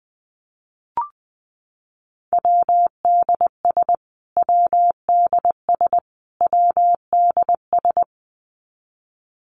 Morse code, heard as a keyed tone, about 20 wpm, 700 Hz. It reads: WDS WDS WDS
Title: Abbreviations Repeated 3x - 20wpm